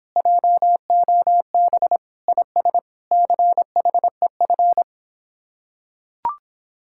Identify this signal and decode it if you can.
Morse code, heard as a keyed tone, about 26 wpm, 700 Hz. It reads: JOB IS CHEF